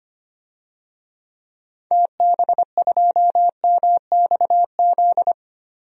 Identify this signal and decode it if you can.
Morse code, heard as a keyed tone, about 25 wpm, 700 Hz. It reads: TB2MXZ
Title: DXCC-100 Call Signs - 6 Characters - Repeated 2x - 25wpm